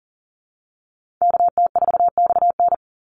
Morse code, heard as a keyed tone, about 40 wpm, 700 Hz. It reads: KT4XN
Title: Call Signs Repeated 3x - 40wpm